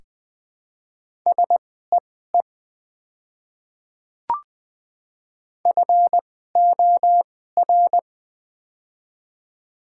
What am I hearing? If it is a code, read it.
Morse code, heard as a keyed tone, about 20 wpm, spaced as for 14 wpm, 700 Hz. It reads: SEE FOR
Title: 2x Character Spacing - Top 100 Words - 20wpm